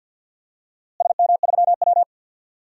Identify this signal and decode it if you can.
Morse code, heard as a keyed tone, about 50 wpm, 700 Hz. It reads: SM3W
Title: DXCC-100 Call Signs - 4 Characters - Repeated 2x - 50wpm